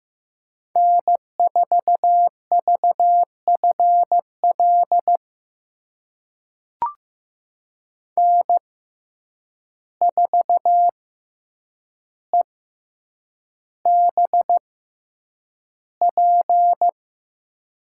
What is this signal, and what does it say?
Morse code, heard as a keyed tone, about 15 wpm, spaced as for 8 wpm, 700 Hz. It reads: N4VFL N4EBP
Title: USA Call Signs - 5 Characters - 6x Character Spacing - 15wpm